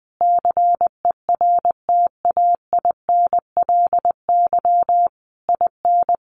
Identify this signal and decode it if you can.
Morse code, heard as a keyed tone, about 20 wpm, 700 Hz. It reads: CERTAINLY IN
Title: Sentences from Top 1500 Words 20wpm